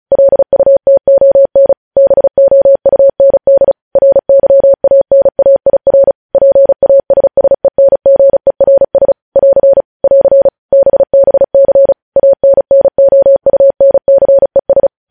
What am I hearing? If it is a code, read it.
Morse code, heard as a keyed tone, about 36 wpm, 550 Hz. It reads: LUTON BOUND RYANAIR PASSENGERS <AR> <AR> BBC ANNOUNCES
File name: podcast / News headlines-1x-35x35